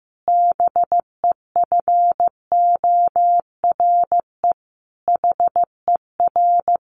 Morse code, heard as a keyed tone, about 15 wpm, 700 Hz. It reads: BEFORE HER
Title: Sentences from Top 1500 Words 15wpm